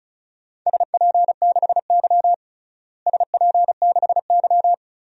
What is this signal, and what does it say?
Morse code, heard as a keyed tone, about 34 wpm, 700 Hz. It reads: SP6Y SP6Y
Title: DXCC-100 Call Signs - 4 Characters - Repeated 2x - 35wpm